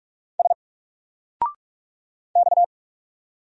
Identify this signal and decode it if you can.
Morse code, heard as a keyed tone, about 44 wpm, 700 Hz. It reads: S X
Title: Single Letter-Number 45wpm